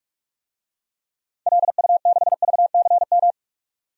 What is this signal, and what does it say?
Morse code, heard as a keyed tone, about 43 wpm, 700 Hz. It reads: LU6VCM
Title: DXCC-100 Call Signs - 6 Characters - 45wpm